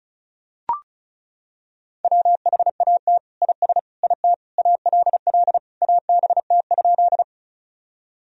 Morse code, heard as a keyed tone, about 36 wpm, 700 Hz. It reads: WHAT IS IT ALL ABT?
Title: Sentences from Top 100 Words and Abbreviations 35wpm